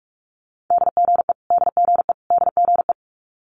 Morse code, heard as a keyed tone, about 46 wpm, 700 Hz. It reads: DGE DGE DGE
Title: Multi-letter Phonemes Repeated 3x 45wpm